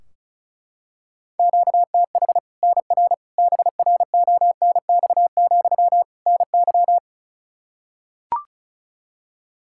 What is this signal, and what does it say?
Morse code, heard as a keyed tone, about 36 wpm, 700 Hz. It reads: QTH NR BRONX, NY
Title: Practice QSO - Exchange 1 - Beginning - 35wpm